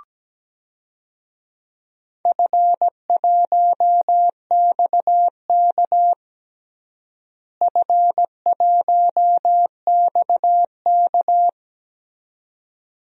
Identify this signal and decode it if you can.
Morse code, heard as a keyed tone, about 17 wpm, 700 Hz. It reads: F1XK F1XK
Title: DXCC-100 Call Signs - 4 Characters - Repeated 2x - 17wpm